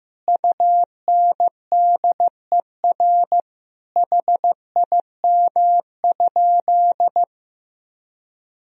Morse code, heard as a keyed tone, about 16 wpm, 700 Hz. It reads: UNDER HIM?
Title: Sentences from Top 1000 Words - Review Part 1 - 15wpm